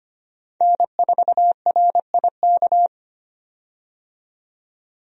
Morse code, heard as a keyed tone, about 25 wpm, 700 Hz. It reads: N4RIK
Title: USA Call Signs - 5 Characters - 25wpm